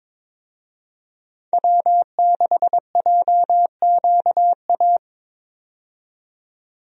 Morse code, heard as a keyed tone, about 22 wpm, 700 Hz. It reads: W6JQA
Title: USA Call Signs - 5 Characters - 22wpm